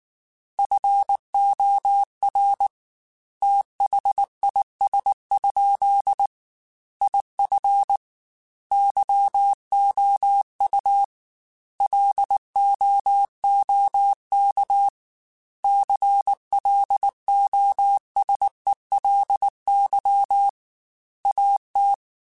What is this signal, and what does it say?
Morse code, heard as a keyed tone, about 19 wpm, 800 Hz. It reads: FOR THIS? IF YOU LOOK CLOSELY AT
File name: Technical Subjects / Technical Subjects-mp3s / OR Gate @18